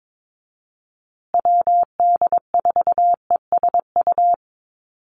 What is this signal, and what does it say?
Morse code, heard as a keyed tone, about 22 wpm, 700 Hz. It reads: WD4ESU
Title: USA Call Signs Repeated 3x 22wpm